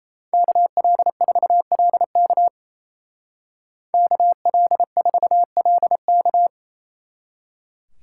Morse code, heard as a keyed tone, about 35 wpm, 700 Hz. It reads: KL4LK KL4LK
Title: Speed Racing - USA Call Signs 28wpm